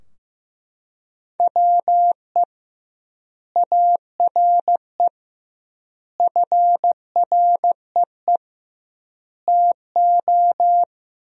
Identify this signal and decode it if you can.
Morse code, heard as a keyed tone, about 15 wpm, 700 Hz. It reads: WE ARE FREE TO
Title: 2x Word-Spacing - Sentences from Top 500 Words - 15wpm